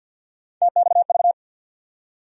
Morse code, heard as a keyed tone, about 49 wpm, 700 Hz. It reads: TXV
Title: USA Call Sign Postfixes - 3 Letters 50wpm